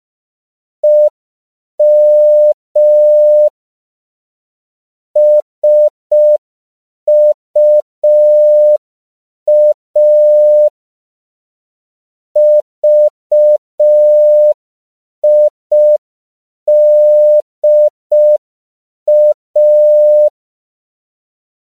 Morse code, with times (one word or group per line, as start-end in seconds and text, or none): 0.8-3.5 EM
5.2-10.7 SUA
12.4-20.3 VIDA